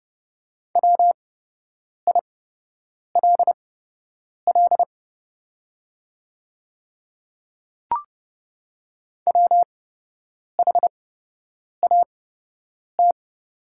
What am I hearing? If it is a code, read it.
Morse code, heard as a keyed tone, about 30 wpm, spaced as for 8 wpm, 700 Hz. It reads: WILL WHAT
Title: 8x Character Spacing - Top 100 Words - 30wpm